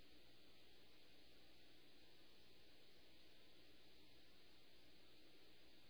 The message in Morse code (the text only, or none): none